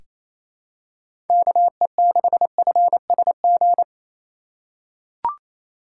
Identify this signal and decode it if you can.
Morse code, heard as a keyed tone, about 28 wpm, 700 Hz. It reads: KE6FSG
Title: USA Call Signs - 6 Characters - 4x Character Spacing - 28wpm